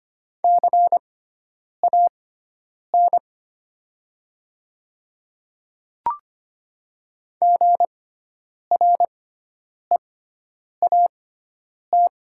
Morse code, heard as a keyed tone, about 25 wpm, spaced as for 9 wpm, 700 Hz. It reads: CAN GREAT
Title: 6x Character Spacing - Top 200 Words - 25wpm